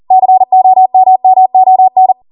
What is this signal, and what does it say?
Morse code, heard as a keyed tone, about 40 wpm, 750 Hz. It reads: COMMON